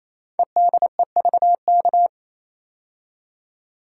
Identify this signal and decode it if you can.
Morse code, heard as a keyed tone, about 27 wpm, 700 Hz. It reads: EDEVK
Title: Code Groups - 5 Letters - 28wpm